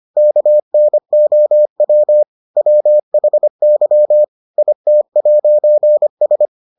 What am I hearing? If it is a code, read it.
Morse code, heard as a keyed tone, about 25 wpm, 600 Hz. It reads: KNOW WHY IT'S